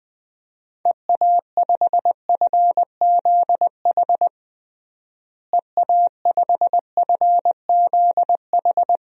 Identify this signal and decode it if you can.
Morse code, heard as a keyed tone, about 20 wpm, 700 Hz. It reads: EA5FZH EA5FZH
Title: DXCC-100 Call Signs - 6 Characters - Repeated 2x - 20wpm